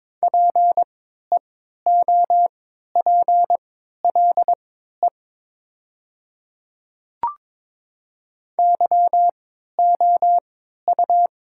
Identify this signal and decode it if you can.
Morse code, heard as a keyed tone, about 22 wpm, spaced as for 12 wpm, 700 Hz. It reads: PEOPLE YOU
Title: 3x Character Spacing - Top 200 Words - 22wpm